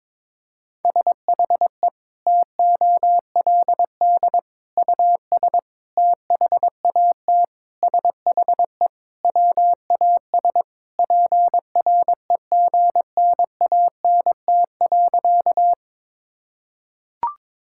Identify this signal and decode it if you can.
Morse code, heard as a keyed tone, about 22 wpm, 700 Hz. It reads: SHE TOLD US THAT SHE WAS PREGNANT.
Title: Sentences from Top 1000 Words - Review Part 6 - 22wpm